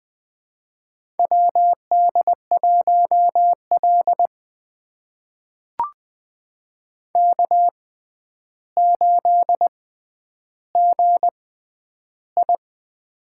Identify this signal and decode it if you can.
Morse code, heard as a keyed tone, about 21 wpm, spaced as for 10 wpm, 700 Hz. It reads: WD1L K8GI
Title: USA Call Signs - 4 Characters - 6x Character Spacing - 20wpm